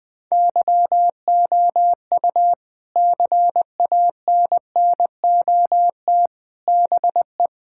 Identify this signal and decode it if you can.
Morse code, heard as a keyed tone, about 20 wpm, 700 Hz. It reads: YOU CANNOT BE
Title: Sentences from Top 1000 Words - Review Part 6 - 20wpm